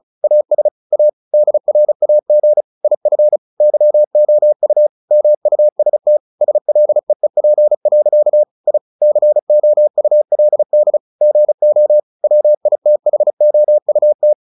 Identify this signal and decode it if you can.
Morse code, heard as a keyed tone, about 34 wpm, 600 Hz. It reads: AS A DRAG IF YOU MUST SLEEP. I COULD GO WITHOUT
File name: the old man and the sea 2 35wpm